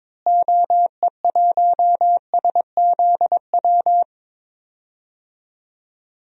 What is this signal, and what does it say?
Morse code, heard as a keyed tone, about 22 wpm, 700 Hz. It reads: OE1SZW
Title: DXCC-100 Call Signs - 6 Characters - Repeated 3x - 22wpm